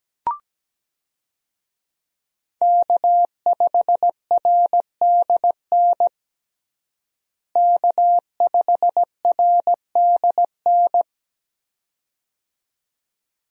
Morse code, heard as a keyed tone, about 17 wpm, 700 Hz. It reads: K5RDN K5RDN
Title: USA Call Signs - 5 Characters - Repeated 2x - 17wpm